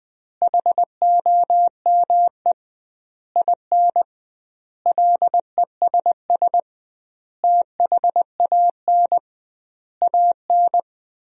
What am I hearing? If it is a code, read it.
Morse code, heard as a keyed tone, about 20 wpm, 700 Hz. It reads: HOME IN LESS THAN AN
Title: 2x Word-Spacing - Sentences from Top 500 Words - 20wpm